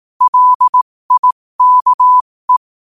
Morse code, heard as a keyed tone, about 18 wpm, spaced as for 15 wpm, 1000 Hz. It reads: LIKE
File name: qotdfiles / 15wpm